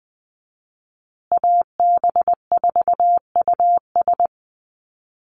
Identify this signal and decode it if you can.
Morse code, heard as a keyed tone, about 20 wpm, 700 Hz. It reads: AB4US
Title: USA Call Signs Repeated 2x 20wpm